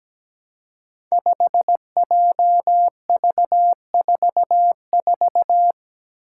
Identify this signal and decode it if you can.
Morse code, heard as a keyed tone, about 17 wpm, 700 Hz. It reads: 5JV44
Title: Five Challenging Letter-Number Combinations 17wpm